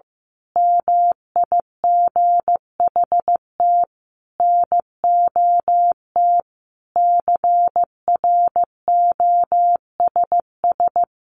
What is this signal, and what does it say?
Morse code, heard as a keyed tone, about 15 wpm, 700 Hz. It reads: MIGHT NOT CROSS
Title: A Sound Of Thunder 15wpm